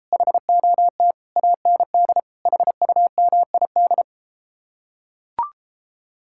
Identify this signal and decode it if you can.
Morse code, heard as a keyed tone, about 33 wpm, 700 Hz. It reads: HOT AND HUMID